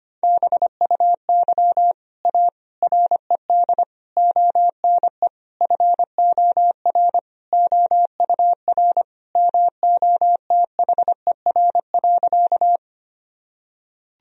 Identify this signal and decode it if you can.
Morse code, heard as a keyed tone, about 25 wpm, 700 Hz. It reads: BUY A RED ONE FOR OUR MOTHER.
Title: Sentences from Top 1000 Words - Review Part 2 - 25wpm